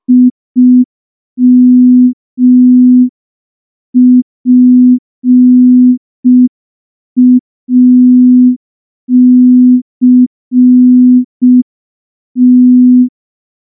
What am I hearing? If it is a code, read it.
Morse code, heard as a keyed tone, about 5 wpm, 250 Hz. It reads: IMPACT
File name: morse / morse-training-05WPM